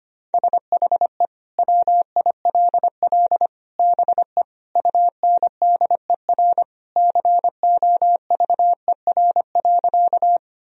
Morse code, heard as a keyed tone, about 25 wpm, 700 Hz. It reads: SHE WILL BE UNDER COVER.
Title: Sentences from Top 1000 Words - Review Part 1 - 25wpm